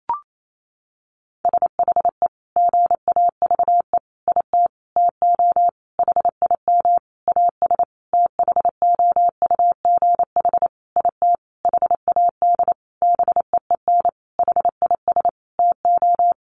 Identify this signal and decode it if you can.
Morse code, heard as a keyed tone, about 28 wpm, 700 Hz. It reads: SHE GAVE IT TO HIM AS THOUGH IT HAD BEEN HIS TO
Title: Sentences from Top 1500 Words 28wpm